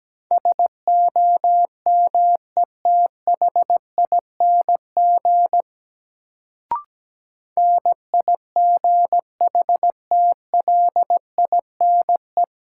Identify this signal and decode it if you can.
Morse code, heard as a keyed tone, about 17 wpm, 700 Hz. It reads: SOMETHING NIGHTLINE
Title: Sets of 5 Words - New Word Drill - 17wpm